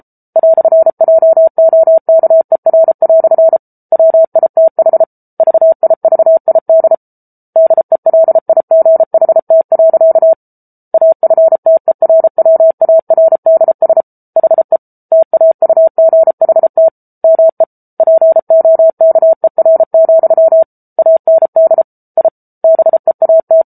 Morse code, heard as a keyed tone, about 34 wpm, 650 Hz. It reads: "JOKER" WITH VIVID DELIGHT. AFTERWARDS HE TAUGHT ME POKER, AND I BEAT